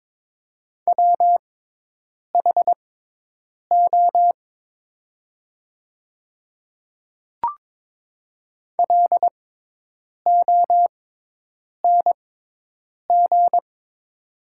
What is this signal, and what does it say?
Morse code, heard as a keyed tone, about 22 wpm, spaced as for 8 wpm, 700 Hz. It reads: WHO LONG
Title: 6x Character Spacing - Top 200 Words - 22wpm